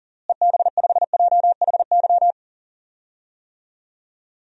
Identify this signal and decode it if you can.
Morse code, heard as a keyed tone, about 40 wpm, 700 Hz. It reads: EB5JHY
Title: DXCC-100 Call Signs - 6 Characters - Repeated 3x - 40wpm